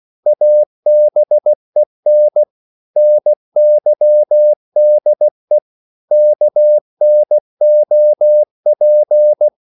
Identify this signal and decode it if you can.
Morse code, heard as a keyed tone, about 16 wpm, 600 Hz. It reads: ABEN NYDE KNOP
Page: Korte danske ord 16wpm